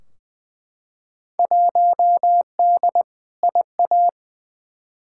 Morse code, heard as a keyed tone, about 20 wpm, 700 Hz. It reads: 1D IA
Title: Field Day Exchanges - 20wpm